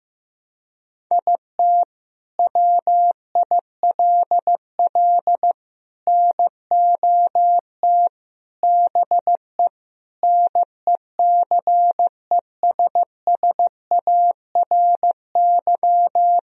Morse code, heard as a keyed tone, about 16 wpm, 700 Hz. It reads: IT WILL NOT BE NECESSARY
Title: Sentences from Top 1000 Words - Review Part 1 - 15wpm